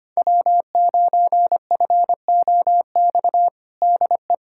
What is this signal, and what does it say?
Morse code, heard as a keyed tone, about 25 wpm, 700 Hz. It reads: W9FOX DE
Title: USA Callsign DE Callsign - 5 Characters - 25wpm